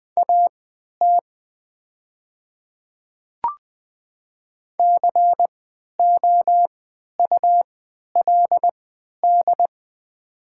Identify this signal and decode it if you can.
Morse code, heard as a keyed tone, about 20 wpm, spaced as for 11 wpm, 700 Hz. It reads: AT COULD
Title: 3x Character Spacing - Top 100 Words - 20wpm